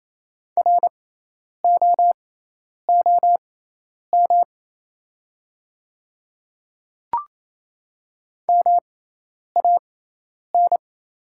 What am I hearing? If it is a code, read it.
Morse code, heard as a keyed tone, about 28 wpm, spaced as for 10 wpm, 690 Hz. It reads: ROOM MAN
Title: 6x Character Spacing - Top 300 Words - 28wpm